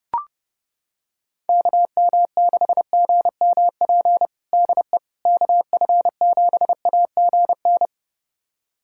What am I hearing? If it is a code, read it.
Morse code, heard as a keyed tone, about 30 wpm, 700 Hz. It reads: KM6GMP DE KF7AGN
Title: USA Callsign DE Callsign - 6 Characters - 30wpm